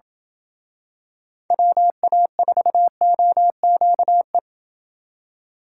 Morse code, heard as a keyed tone, about 27 wpm, 700 Hz. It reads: WA4OQE